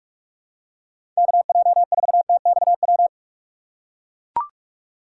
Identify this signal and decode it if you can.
Morse code, heard as a keyed tone, about 42 wpm, 700 Hz. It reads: KJ4TXW
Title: USA Call Signs - 6 Characters - 6x Character Spacing - 45wpm